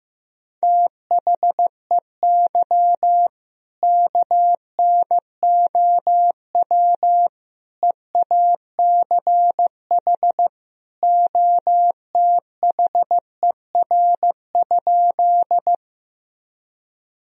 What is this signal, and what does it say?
Morse code, heard as a keyed tone, about 15 wpm, 700 Hz. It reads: THEY KNOW EACH OTHER?
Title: Sentences from Top 1000 Words - Review Part 1 - 15wpm